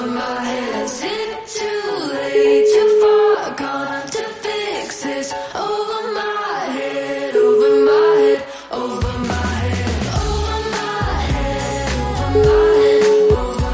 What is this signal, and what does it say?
Ringback tone (Europe, 425 Hz), with noise.